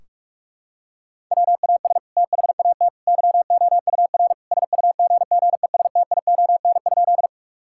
Morse code, heard as a keyed tone, about 46 wpm, 700 Hz. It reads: WAS THAT YOUR SUGGESTION?
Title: Sentences from Top 1600 Words 45wpm